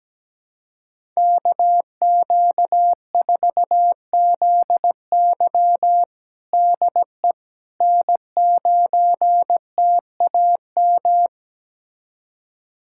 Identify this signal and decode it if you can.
Morse code, heard as a keyed tone, about 17 wpm, 700 Hz. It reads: KQ4ZY DE N9TAM
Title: USA Callsign DE Callsign - 5 Characters - 17wpm